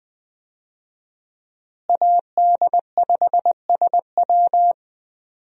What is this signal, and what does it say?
Morse code, heard as a keyed tone, about 21 wpm, 700 Hz. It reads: AD5SW
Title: USA Call Signs - 5 Characters - Repeated 3x - 20wpm